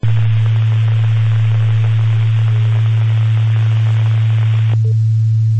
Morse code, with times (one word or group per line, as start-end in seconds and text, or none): none